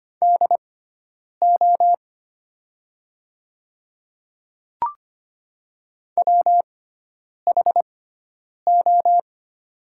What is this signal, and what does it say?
Morse code, heard as a keyed tone, about 25 wpm, spaced as for 9 wpm, 700 Hz. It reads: DO WHO
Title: 6x Character Spacing - Top 100 Words - 25wpm